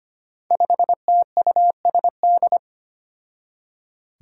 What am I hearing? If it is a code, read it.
Morse code, heard as a keyed tone, about 25 wpm, 700 Hz. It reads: 5TUSD
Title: Five Challenging Letter-Number Combinations 25wpm